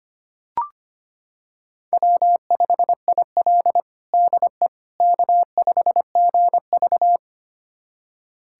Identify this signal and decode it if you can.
Morse code, heard as a keyed tone, about 25 wpm, 700 Hz. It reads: W5IL DE K5GV
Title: USA Callsign DE Callsign - 4 Characters - 25wpm